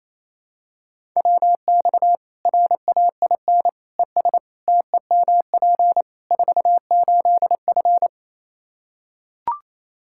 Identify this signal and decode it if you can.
Morse code, heard as a keyed tone, about 28 wpm, 700 Hz. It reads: WX RAIN ES TEMP 48F
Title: Practice QSO - Exchange 2 - Info - 28wpm